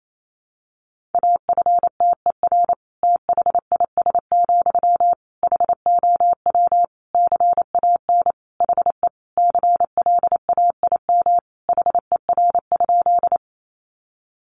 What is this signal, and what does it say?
Morse code, heard as a keyed tone, about 28 wpm, 700 Hz. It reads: AFTER THIS, HOW CAN HE CLAIM HER?
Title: Sentences from Top 1500 Words 28wpm